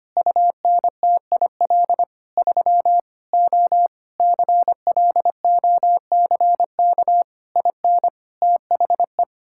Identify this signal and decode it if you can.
Morse code, heard as a keyed tone, about 25 wpm, 700 Hz. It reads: UNTIL 3 O CLOCK IN THE
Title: Pangram Challenge v2 - 25wpm